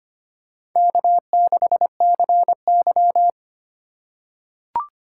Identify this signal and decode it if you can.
Morse code, heard as a keyed tone, about 25 wpm, 700 Hz. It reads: K6CY
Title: USA Call Signs - 4 Characters - 3x Character Spacing - 25wpm